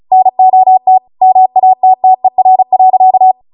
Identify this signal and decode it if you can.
Morse code, heard as a keyed tone, about 36 wpm, 750 Hz. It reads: NOT MATTER.